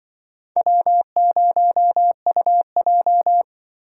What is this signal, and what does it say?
Morse code, heard as a keyed tone, about 25 wpm, 700 Hz. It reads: W0UJ
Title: Speed Racing - USA Call Signs 20wpm